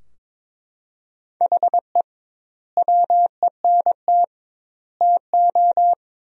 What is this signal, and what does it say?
Morse code, heard as a keyed tone, about 22 wpm, 700 Hz. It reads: HE WENT TO
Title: 2x Word-Spacing - Sentences from Top 200 Words - 22wpm